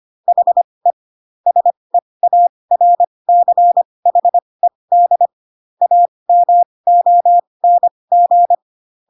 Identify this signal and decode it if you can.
Morse code, heard as a keyed tone, about 25 wpm, spaced as for 20 wpm, 700 Hz. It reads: HE SEARCHED AMONG
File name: Pinocchio / Pinocchio ch1 15 25 ik1hge